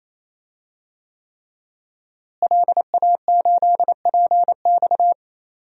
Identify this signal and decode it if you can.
Morse code, heard as a keyed tone, about 28 wpm, 700 Hz. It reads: LA8PX
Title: DXCC-100 Call Signs - 5 Characters - Repeated 3x - 28wpm